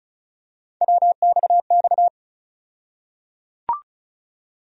Morse code, heard as a keyed tone, about 36 wpm, 700 Hz. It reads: WXX
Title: USA Call Sign Postfixes - 3 Letters 35wpm